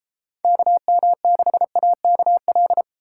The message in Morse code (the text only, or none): KM6AKL